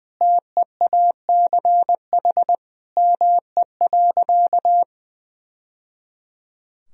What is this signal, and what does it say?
Morse code, heard as a keyed tone, about 20 wpm, 700 Hz. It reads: TEACH ME.
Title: Sentences from Top 1000 Words - Review Part 2 - 20wpm